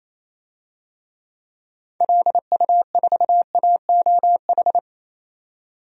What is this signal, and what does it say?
Morse code, heard as a keyed tone, about 27 wpm, 700 Hz. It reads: LU4AOH
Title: DXCC-100 Call Signs - 6 Characters - 28wpm